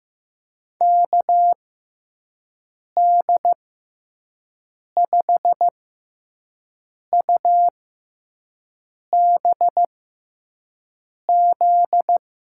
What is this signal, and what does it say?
Morse code, heard as a keyed tone, about 15 wpm, spaced as for 5 wpm, 700 Hz. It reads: KD5UBZ
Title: USA Call Signs - 6x Character Spacing - 15wpm